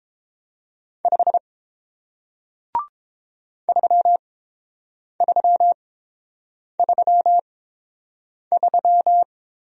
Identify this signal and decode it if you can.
Morse code, heard as a keyed tone, about 30 wpm, spaced as for 7 wpm, 700 Hz. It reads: 5 3333